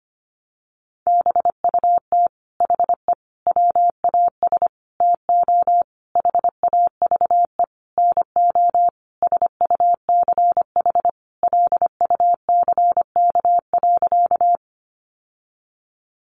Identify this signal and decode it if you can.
Morse code, heard as a keyed tone, about 25 wpm, 700 Hz. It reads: BUT HE WAS TO HAVE NO SUCH LUCK.
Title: Sentences from Top 1500 Words 25wpm